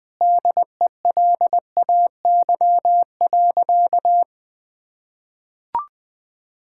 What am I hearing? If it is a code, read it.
Morse code, heard as a keyed tone, about 21 wpm, 700 Hz. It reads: DELAY.
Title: Sentences with 2-Word Phrases - 20wpm